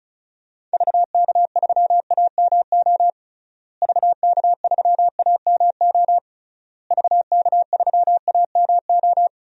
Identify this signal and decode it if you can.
Morse code, heard as a keyed tone, about 34 wpm, 700 Hz. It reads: VK3AMO VK3AMO VK3AMO
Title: DXCC-100 Call Signs - 6 Characters - Repeated 3x - 35wpm